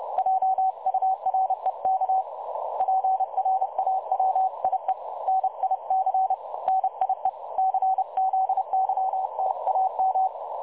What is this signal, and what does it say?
Morse code, heard as a keyed tone, about 31 wpm, 760 Hz, with radio noise. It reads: JUREK PRAWIE NIC NIE CZYTAM
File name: dlazaawansowanych / nagraniazpasma / websdr recording 3565.0kHz